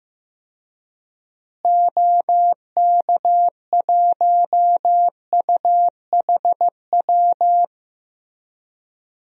Morse code, heard as a keyed tone, about 16 wpm, 700 Hz. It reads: OK1UHW
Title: DXCC-100 Call Signs - 6 Characters - Repeated 3x - 15wpm